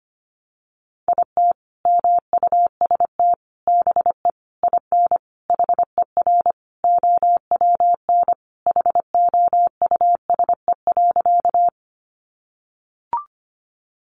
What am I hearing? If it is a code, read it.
Morse code, heard as a keyed tone, about 25 wpm, 700 Hz. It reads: IT MUST BE IN HER OWN HOUSE.
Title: Sentences from Top 300 Words 25wpm